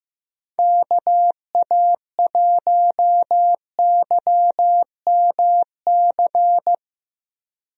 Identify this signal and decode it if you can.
Morse code, heard as a keyed tone, about 15 wpm, 700 Hz. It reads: KA1YMC
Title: USA Call Signs - 6 Characters - Repeated 3x - 15wpm